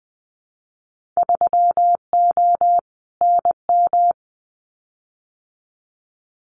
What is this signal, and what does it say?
Morse code, heard as a keyed tone, about 20 wpm, 700 Hz. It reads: 3O NM
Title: Winter Field Day Exchanges Encore 20wpm